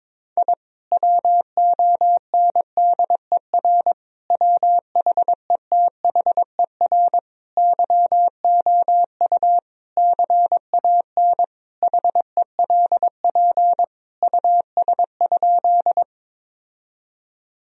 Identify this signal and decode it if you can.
Morse code, heard as a keyed tone, about 22 wpm, 700 Hz. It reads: I WONDER WHETHER YOU CAN HELP US?
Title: Sentences from Top 1000 Words - Review Part 3 - 22wpm